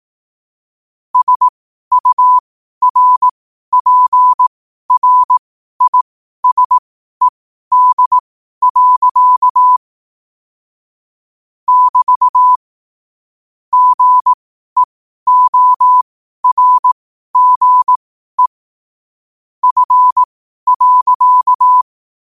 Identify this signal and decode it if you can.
Morse code, heard as a keyed tone, about 18 wpm, spaced as for 12 wpm, 1000 Hz. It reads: SURPRISED. <BT> GEORGE F.